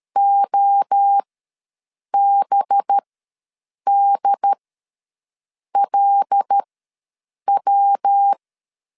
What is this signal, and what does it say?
Morse code, heard as a keyed tone, about 13 wpm, spaced as for 7 wpm, 800 Hz. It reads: OBDLW
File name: LEZIONI / 5 LBVC / 5 LBVC ws 12wpm 3159